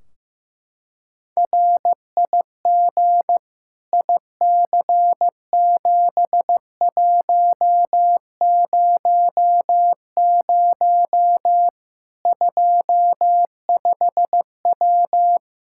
Morse code, heard as a keyed tone, about 15 wpm, 700 Hz. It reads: RIG IC7100 25W
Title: Practice QSO - Exchange 2 - Info - 15wpm